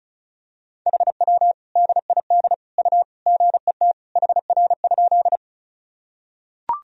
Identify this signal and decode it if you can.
Morse code, heard as a keyed tone, about 34 wpm, 690 Hz. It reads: HW DID U GET HR?
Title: Sentences from Top 200 Words and Abbreviations 35wpm